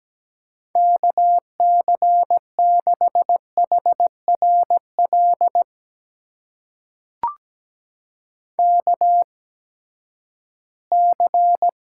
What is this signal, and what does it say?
Morse code, heard as a keyed tone, about 17 wpm, spaced as for 12 wpm, 700 Hz. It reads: KC6HRL KC
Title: USA Call Signs - 6 Characters - 8x Character Spacing - 17wpm